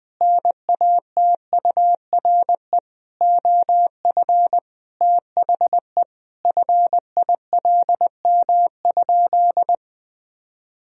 Morse code, heard as a keyed tone, about 20 wpm, 700 Hz. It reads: NATURE OF THE FILM?
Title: Sentences from Top 1000 Words - Review Part 3 - 20wpm